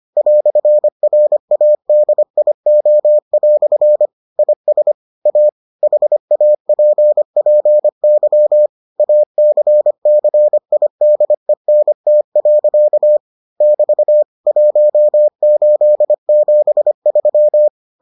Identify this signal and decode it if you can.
Morse code, heard as a keyed tone, about 25 wpm, 600 Hz. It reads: "RADIO" IS A HAPPY ACCIDENT. <BT> 1873